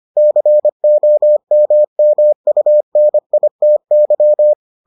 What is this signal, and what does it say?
Morse code, heard as a keyed tone, about 25 wpm, 600 Hz. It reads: COMMUNITY